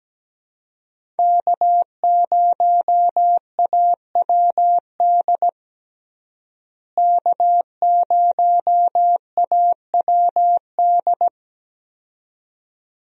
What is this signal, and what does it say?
Morse code, heard as a keyed tone, about 17 wpm, 700 Hz. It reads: K0AWD K0AWD
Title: USA Call Signs - 5 Characters - Repeated 2x - 17wpm